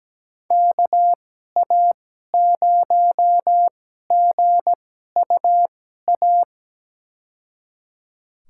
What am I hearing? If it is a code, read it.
Morse code, heard as a keyed tone, about 17 wpm, spaced as for 12 wpm, 700 Hz. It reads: KA0GUA
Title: USA Call Signs - 6 Characters - 2x Character Spacing - 17wpm